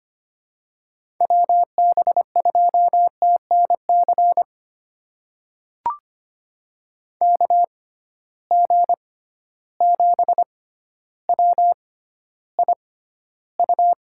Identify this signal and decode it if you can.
Morse code, heard as a keyed tone, about 25 wpm, spaced as for 13 wpm, 700 Hz. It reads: WB2TNC KG7WIU
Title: USA Call Signs - 6 Characters - 6x Character Spacing - 25wpm